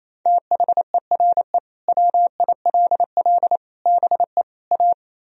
Morse code, heard as a keyed tone, about 35 wpm, 700 Hz. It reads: THERE WILL BE A
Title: Speed-Racing - Sets of 4 Words 28wpm